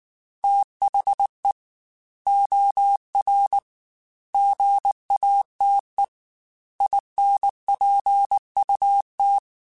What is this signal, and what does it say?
Morse code, heard as a keyed tone, about 19 wpm, 800 Hz. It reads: THE OR GATE INPUT
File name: Technical Subjects / Technical Subjects-mp3s / OR Gate @18